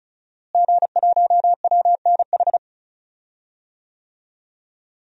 Morse code, heard as a keyed tone, about 35 wpm, 700 Hz. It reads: G1WNH
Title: DXCC-100 Call Signs - 5 Characters - Repeated 3x - 35wpm